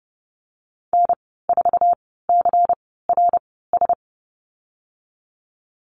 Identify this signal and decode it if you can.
Morse code, heard as a keyed tone, about 31 wpm, spaced as for 17 wpm, 700 Hz. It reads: N4CRS
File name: Lesson 19g - Letter w - Call Signs 30wpm